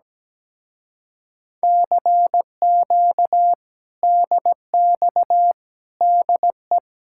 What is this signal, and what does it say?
Morse code, heard as a keyed tone, about 17 wpm, 700 Hz. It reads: CQ DX DE
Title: Speed-Racing Common QSO Phrases - 17wpm